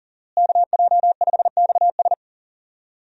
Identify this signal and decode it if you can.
Morse code, heard as a keyed tone, about 40 wpm, 700 Hz. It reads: KJ5XS
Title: USA Call Signs - 5 Characters - Repeated 2x - 40wpm